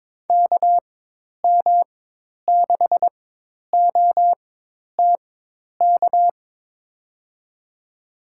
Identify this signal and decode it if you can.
Morse code, heard as a keyed tone, about 22 wpm, spaced as for 10 wpm, 700 Hz. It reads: KM6OTK
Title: USA Call Signs - 6 Characters - 4x Character Spacing - 22wpm